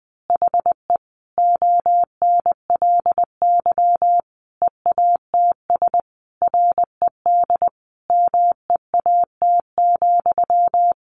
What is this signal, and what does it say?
Morse code, heard as a keyed tone, about 20 wpm, 700 Hz. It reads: HE ONLY EATS RED MEAT,
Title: Easy Phrases 20wpm